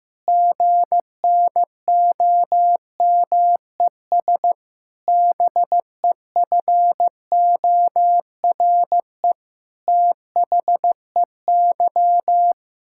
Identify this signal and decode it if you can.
Morse code, heard as a keyed tone, about 15 wpm, 700 Hz. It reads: GNOMES BEFORE THEY
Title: Pangram Challenge v2 - 15wpm